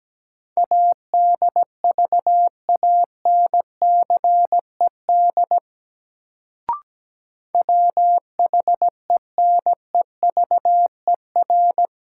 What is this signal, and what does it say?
Morse code, heard as a keyed tone, about 17 wpm, 700 Hz. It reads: ADVANCED WHENEVER
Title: Sentences from Top 1200 Words - New Word Drill Alt - 17wpm